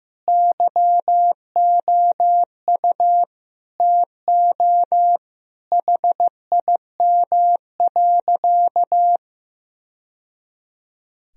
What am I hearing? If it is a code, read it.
Morse code, heard as a keyed tone, about 16 wpm, 700 Hz. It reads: YOU TO HIM.
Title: Sentences from Top 1000 Words - Review Part 2 - 15wpm